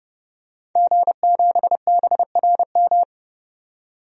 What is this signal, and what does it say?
Morse code, heard as a keyed tone, about 32 wpm, 700 Hz. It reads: G7BRM
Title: DXCC-100 Call Signs - 5 Characters - 30wpm